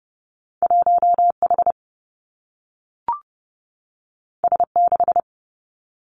Morse code, heard as a keyed tone, about 30 wpm, 700 Hz. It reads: 1H S6
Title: Two Challenging Letter-Number Combinations 30wpm